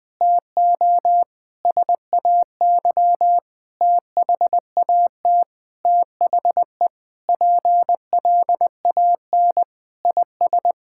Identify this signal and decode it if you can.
Morse code, heard as a keyed tone, about 21 wpm, 700 Hz. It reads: TO SAY THAT THE PLAN IS
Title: Sentences with 7-Word Phrases - 20wpm